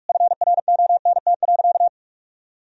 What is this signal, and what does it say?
Morse code, heard as a keyed tone, about 46 wpm, 700 Hz. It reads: FRONT.
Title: Sentences from Top 1000 Words - Review Part 2 - 45wpm